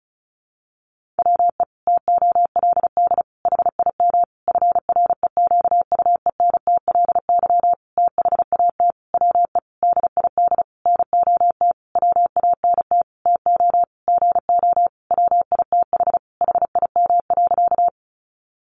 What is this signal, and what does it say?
Morse code, heard as a keyed tone, about 34 wpm, 700 Hz. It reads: WE TOLD HIM FREQUENTLY THAT WE DID NOT WANT TO GO WITH HIM.
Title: Sentences from Top 1300 Words 35wpm